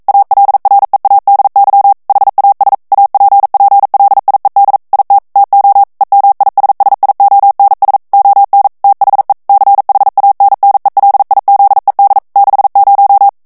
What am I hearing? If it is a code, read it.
Morse code, heard as a keyed tone, about 42 wpm, 800 Hz. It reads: ALREADY HAS APPLIED IT TO EMISSIONS ON THE CHANNELIZED 60